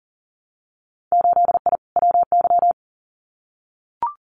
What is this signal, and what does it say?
Morse code, heard as a keyed tone, about 37 wpm, 700 Hz. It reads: 8I WY
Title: Winter Field Day Exchanges Encore 40wpm